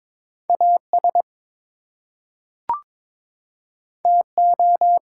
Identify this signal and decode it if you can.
Morse code, heard as a keyed tone, about 22 wpm, 700 Hz. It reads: AS TO
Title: Two Letter Words - 22wpm